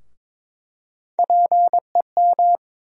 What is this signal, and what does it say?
Morse code, heard as a keyed tone, about 22 wpm, 700 Hz. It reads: PEM
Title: USA Call Sign Postfixes - 3 Letters 22wpm